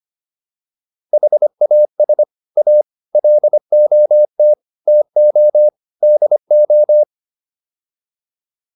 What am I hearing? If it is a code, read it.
Morse code, heard as a keyed tone, about 25 wpm, 600 Hz. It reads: HAS A LOT TO DO